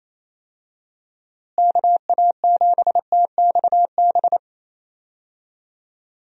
Morse code, heard as a keyed tone, about 28 wpm, 700 Hz. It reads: KA7TXB
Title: USA Call Signs - 6 Characters - Repeated 2x - 28wpm